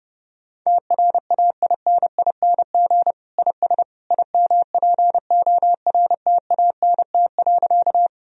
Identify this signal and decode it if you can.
Morse code, heard as a keyed tone, about 29 wpm, 700 Hz. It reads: TRAINING IS IMPORTANT.
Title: Sentences from Top 1000 Words - Review Part 4 - 30wpm